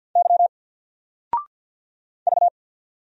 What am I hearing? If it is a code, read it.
Morse code, heard as a keyed tone, about 50 wpm, 700 Hz. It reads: Y V
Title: Single Character (Letter:Number:Punctuation:Prosign) 50wpm